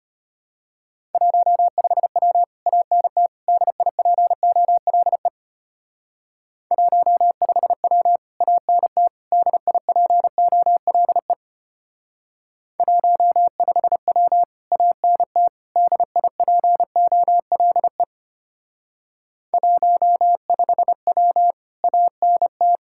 Morse code, heard as a keyed tone, about 30 wpm, 700 Hz. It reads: 15W ANT DIPOLE 15W ANT DIPOLE 15W ANT DIPOLE 15W ANT
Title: Speed-Racing Common QSO Phrases Encore - 25wpm